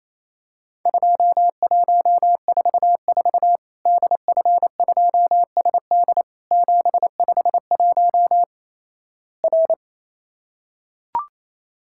Morse code, heard as a keyed tone, about 28 wpm, 700 Hz. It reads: 2144 DF2SD 751 R
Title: Worked All Europe DX Contest - QTC Exchange - 28wpm